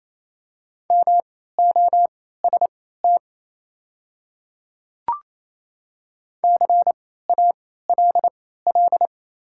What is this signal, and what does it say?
Morse code, heard as a keyed tone, about 28 wpm, spaced as for 16 wpm, 690 Hz. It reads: MOST CALL
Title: 3x Character Spacing - Top 100 Words - 28wpm